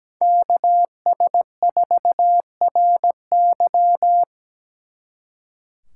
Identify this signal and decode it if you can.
Morse code, heard as a keyed tone, about 20 wpm, 700 Hz. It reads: KS4RY
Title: Speed Racing - USA 2x2 Call Signs 17wpm